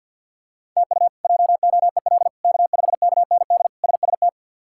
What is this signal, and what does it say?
Morse code, heard as a keyed tone, about 49 wpm, 700 Hz. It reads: TU JOEL K5KND SST
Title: K1USN SST QSO Practice - 50wpm